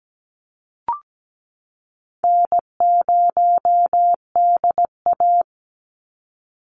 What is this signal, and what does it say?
Morse code, heard as a keyed tone, about 17 wpm, 700 Hz. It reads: N0DA
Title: NA SOTA Chaser Callsigns 17wpm